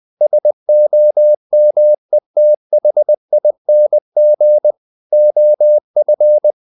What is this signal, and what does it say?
Morse code, heard as a keyed tone, about 20 wpm, 600 Hz. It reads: SOMETHING OF
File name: the old man and the sea / the old man and the sea part 1 20wpm